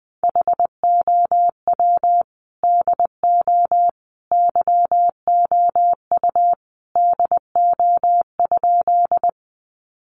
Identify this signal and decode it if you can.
Morse code, heard as a keyed tone, about 20 wpm, 700 Hz. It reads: HOW DO YOU DO?
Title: Sentences from Top 100 Words 20wpm